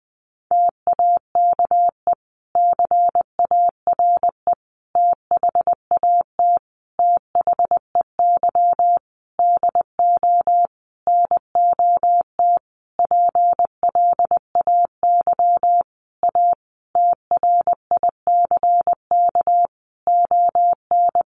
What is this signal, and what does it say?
Morse code, heard as a keyed tone, about 20 wpm, 700 Hz. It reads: TAKE CARE THAT THEY DO NOT PLAY A TRICK ON
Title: Sentences from Top 1500 Words 20wpm